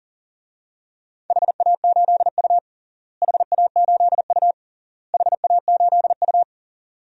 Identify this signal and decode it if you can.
Morse code, heard as a keyed tone, about 40 wpm, 700 Hz. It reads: HA8U HA8U HA8U
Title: DXCC-100 Call Signs - 4 Characters - Repeated 3x - 40wpm